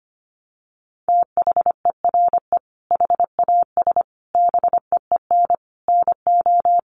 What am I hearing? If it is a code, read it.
Morse code, heard as a keyed tone, about 25 wpm, 700 Hz. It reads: THERE HAS BEEN NO
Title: Sentences from Top 1500 Words 25wpm